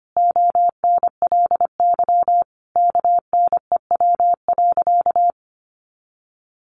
Morse code, heard as a keyed tone, about 25 wpm, 700 Hz. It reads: ONLY KNEW.
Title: Sentences from Top 400 Words 25wpm